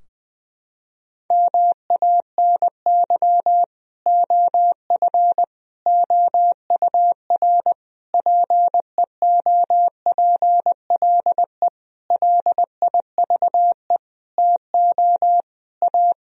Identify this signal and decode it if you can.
Morse code, heard as a keyed tone, about 20 wpm, 700 Hz. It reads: MANY OF OUR PEOPLE LIVE TO A
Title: Sentences from Top 1000 Words - Review Part 3 - 20wpm